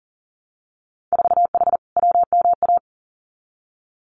Morse code, heard as a keyed tone, about 40 wpm, 700 Hz. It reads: 4H WMA
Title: Winter Field Day Exchanges Encore 40wpm